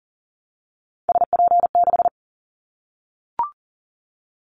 Morse code, heard as a keyed tone, about 40 wpm, 700 Hz. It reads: SP6
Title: Callsign Prefixes for Top 25 DXCC - Repeated 3x - 40wpm